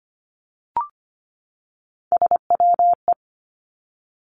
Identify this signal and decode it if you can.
Morse code, heard as a keyed tone, about 25 wpm, 700 Hz. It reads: SWE
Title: Three Challenging Letter-Number Combinations 25wpm